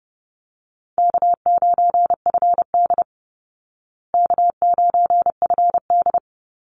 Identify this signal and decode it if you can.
Morse code, heard as a keyed tone, about 28 wpm, 700 Hz. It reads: K9FD K9FD
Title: Call Signs Repeated 2x - 30wpm